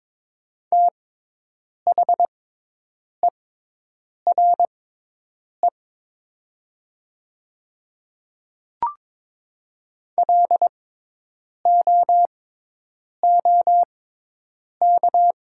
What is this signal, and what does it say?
Morse code, heard as a keyed tone, about 22 wpm, spaced as for 8 wpm, 700 Hz. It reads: THERE LOOK
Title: 6x Character Spacing - Top 200 Words - 22wpm